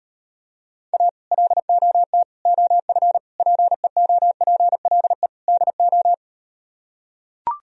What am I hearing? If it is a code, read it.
Morse code, heard as a keyed tone, about 38 wpm, 700 Hz. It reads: A LOT OF PEOPLE DO